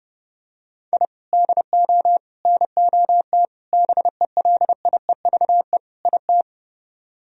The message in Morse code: I DO NOT BELIEVE IT

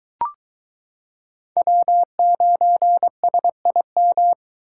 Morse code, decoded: W9SIM

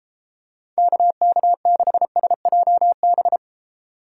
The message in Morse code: KK6SJB

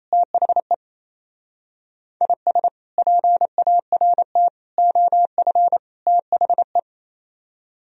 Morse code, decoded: THE IS PART OF THE

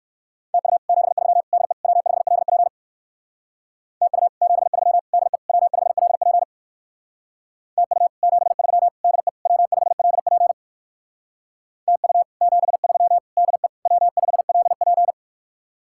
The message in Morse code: TU 73 DE W5LP TU 73 DE W5LP TU 73 DE W5LP TU 73 DE W5LP

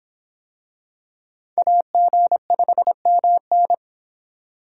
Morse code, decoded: AG5MN

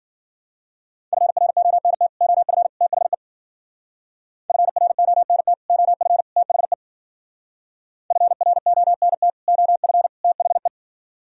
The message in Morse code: FRONT OF THE FRONT OF THE FRONT OF THE

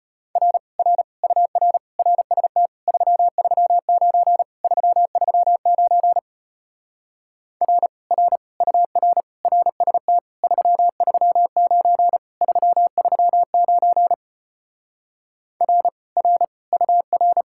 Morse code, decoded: R R UR RST 339 339 R R UR RST 339 339 R R UR

